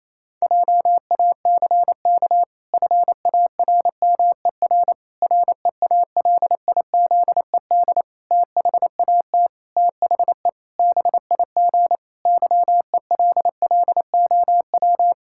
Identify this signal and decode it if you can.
Morse code, heard as a keyed tone, about 27 wpm, 700 Hz. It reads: JACK FARMER REALIZED THAT THE BIG YELLOW
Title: Pangram Challenge v2 - 28wpm